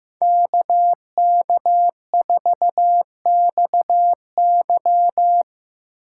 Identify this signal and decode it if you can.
Morse code, heard as a keyed tone, about 15 wpm, 700 Hz. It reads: KK4XY
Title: USA Call Signs - 3x Character Spacing - 15wpm